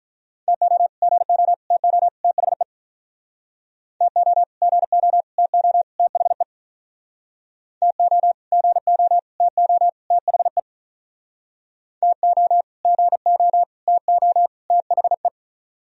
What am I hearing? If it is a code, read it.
Morse code, heard as a keyed tone, about 45 wpm, 700 Hz. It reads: TO GO TO THE TO GO TO THE TO GO TO THE TO GO TO THE